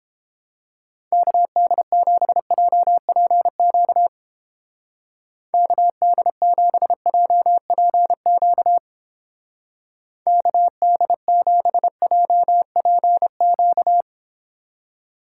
Morse code, heard as a keyed tone, about 29 wpm, 700 Hz. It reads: KD7JPQ KD7JPQ KD7JPQ